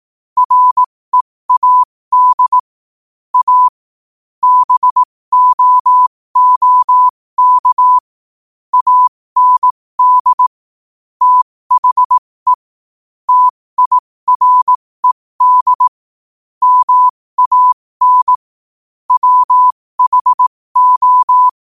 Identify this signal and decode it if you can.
Morse code, heard as a keyed tone, about 18 wpm, spaced as for 15 wpm, 1000 Hz. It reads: READ A BOOK AND THE TIRED MAN WHO
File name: qotdfiles / 15wpm